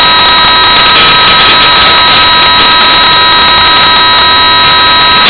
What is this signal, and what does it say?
PSK31, about 31.25 baud.